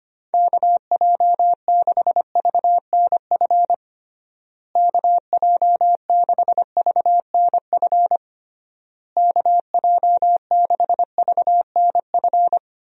Morse code, heard as a keyed tone, about 25 wpm, 700 Hz. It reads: KJ6VNF KJ6VNF KJ6VNF
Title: USA Call Signs - 6 Characters - Repeated 3x - 25wpm